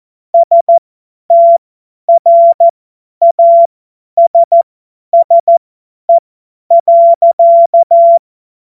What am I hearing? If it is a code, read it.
Morse code, heard as a keyed tone, about 14 wpm, spaced as for 10 wpm, 680 Hz. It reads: STRASSE.
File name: Morsekurs / Witze / Witz 22. August 2023 50 F70